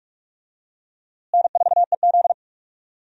Morse code, heard as a keyed tone, about 46 wpm, 700 Hz. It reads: N4EZ
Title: USA Call Signs - 4 Characters - Repeated 3x - 45wpm